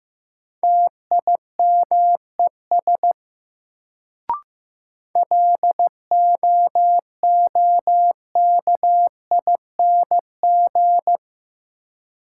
Morse code, heard as a keyed tone, about 15 wpm, 700 Hz. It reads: TIMES LOOKING
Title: Sentences from Top 1000 Words - Review Part 6 - New Word Drill - 15wpm